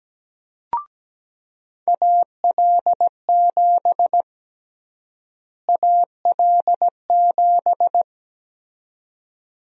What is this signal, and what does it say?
Morse code, heard as a keyed tone, about 17 wpm, 700 Hz. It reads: AL7 AL7
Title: USA Call Sign Prefixes - Repeated 2x - 17wpm